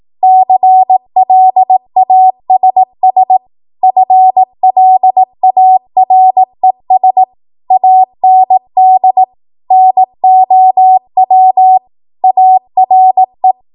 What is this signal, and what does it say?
Morse code, heard as a keyed tone, about 18 wpm, 750 Hz. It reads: CLASS FLARES AND NOW ARE
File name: Archive / 18 WPM / 18WPM